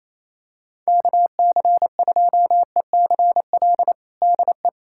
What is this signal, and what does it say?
Morse code, heard as a keyed tone, about 28 wpm, 700 Hz. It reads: KC2ECL DE
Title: USA Callsign DE Callsign - 6 Characters - 28wpm